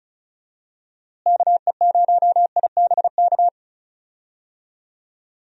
Morse code, heard as a keyed tone, about 34 wpm, 700 Hz. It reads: KE0IBK
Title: USA Call Signs - 6 Characters - Repeated 2x - 35wpm